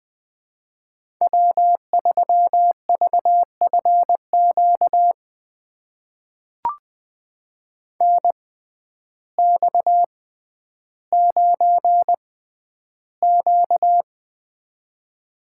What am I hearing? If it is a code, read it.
Morse code, heard as a keyed tone, about 20 wpm, spaced as for 11 wpm, 700 Hz. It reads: W3VFQ NX9Q
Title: USA Call Signs - 6x Character Spacing - 20wpm